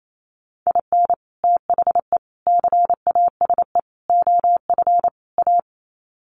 Morse code, heard as a keyed tone, about 28 wpm, 700 Hz. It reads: IN THE CASE OF A
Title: Sets of 5 Words Encore 28wpm